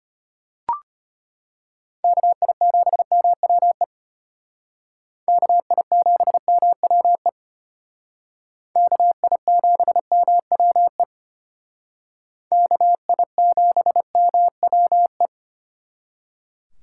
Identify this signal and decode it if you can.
Morse code, heard as a keyed tone, about 29 wpm, 700 Hz. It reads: KI7MWE KI7MWE KI7MWE KI7MWE
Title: Speed Racing - USA 2x3 Call Signs 25wpm